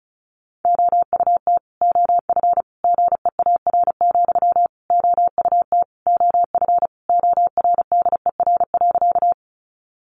Morse code, heard as a keyed tone, about 34 wpm, 700 Hz. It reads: OUT OF GEAR, OUT OF ORDER.
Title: Sentences from Top 1400 Words 35wpm